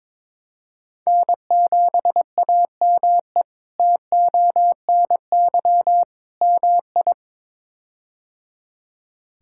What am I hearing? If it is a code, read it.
Morse code, heard as a keyed tone, about 22 wpm, 700 Hz. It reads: N7AME TONY MI
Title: K1USN SST QSO Practice - 22wpm